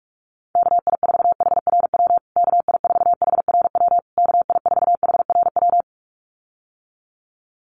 Morse code, heard as a keyed tone, about 44 wpm, 700 Hz. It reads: KI4HRW KI4HRW KI4HRW
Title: USA Call Signs Repeated 3x 45wpm